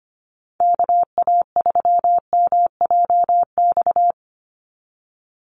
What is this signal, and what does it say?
Morse code, heard as a keyed tone, about 25 wpm, 700 Hz. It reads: KA3MJX
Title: USA Call Signs 25wpm